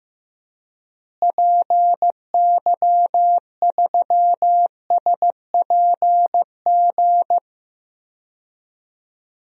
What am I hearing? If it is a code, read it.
Morse code, heard as a keyed tone, about 15 wpm, 700 Hz. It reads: PY3SPG
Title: DXCC-100 Call Signs - 6 Characters - Repeated 2x - 15wpm